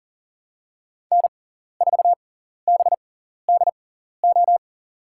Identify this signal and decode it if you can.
Morse code, heard as a keyed tone, about 42 wpm, spaced as for 14 wpm, 700 Hz. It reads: N4BDO
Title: USA Call Signs - 5 Characters - 6x Character Spacing - 40wpm